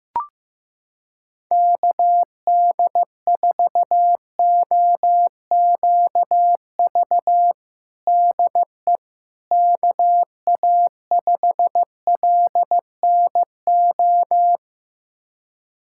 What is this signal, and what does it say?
Morse code, heard as a keyed tone, about 16 wpm, 700 Hz. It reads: KD4OQV DE KA5LNO
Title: USA Callsign DE Callsign - 6 Characters - 15wpm